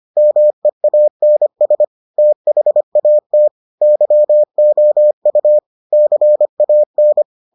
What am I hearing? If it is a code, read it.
Morse code, heard as a keyed tone, about 25 wpm, 600 Hz. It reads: MEANS THAT YOU CAN